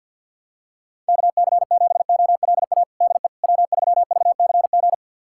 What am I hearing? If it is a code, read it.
Morse code, heard as a keyed tone, about 47 wpm, 700 Hz. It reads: KC7OLA DE W3VCG
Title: USA Callsign DE Callsign 50wpm